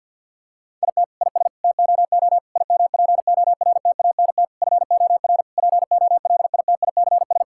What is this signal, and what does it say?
Morse code, heard as a keyed tone, about 50 wpm, 700 Hz. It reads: IT IS TOO IMPORTANT FOR POLITICS
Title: Sentences from Top 1000 Words - Review Part 6 - 50wpm